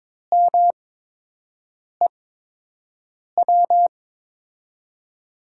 Morse code, heard as a keyed tone, about 22 wpm, spaced as for 6 wpm, 700 Hz. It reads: MEW
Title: USA Call Signs - 5 Characters - 8x Character Spacing - 22wpm